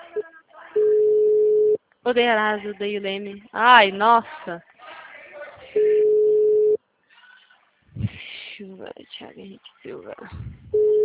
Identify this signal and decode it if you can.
Ringback tone (Europe, 425 Hz), with speech.